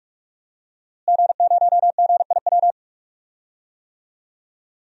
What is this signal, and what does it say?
Morse code, heard as a keyed tone, about 45 wpm, 700 Hz. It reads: G0GIW
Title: DXCC-100 Call Signs - 5 Characters - Repeated 3x - 45wpm